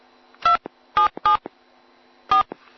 DTMF, heard as telephone touch tones, tones of about 120 ms. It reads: 277 4